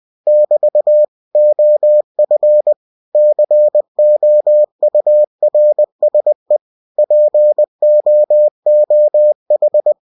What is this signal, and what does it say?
Morse code, heard as a keyed tone, about 20 wpm, 600 Hz. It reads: <BT> OF COURSE POOH
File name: cw-books / winnie-the-pooh / winnie-20wpm-page-0041